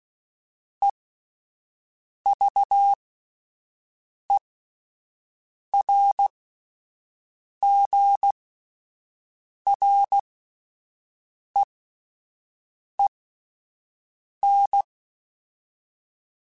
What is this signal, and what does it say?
Morse code, heard as a keyed tone, about 16 wpm, spaced as for 5 wpm, 800 Hz. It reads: EVERGREEN